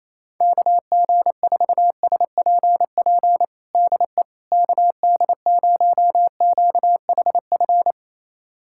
Morse code, heard as a keyed tone, about 28 wpm, 700 Hz. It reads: KG4SPP DE KD0QHF
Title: USA Callsign DE Callsign - 6 Characters - 28wpm